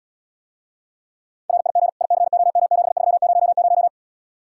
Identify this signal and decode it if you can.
Morse code, heard as a keyed tone, about 79 wpm, 700 Hz. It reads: RIG ICOM7300